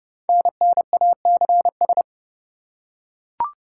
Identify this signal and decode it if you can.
Morse code, heard as a keyed tone, about 30 wpm, 700 Hz. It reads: NNACS
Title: Code Groups - 5 Letters - 30wpm